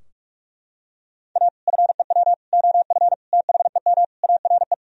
Morse code, heard as a keyed tone, about 41 wpm, 700 Hz. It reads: A FEW OF THEM ARE
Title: Sentences from Top 1000 Words - Review Part 3 - 45wpm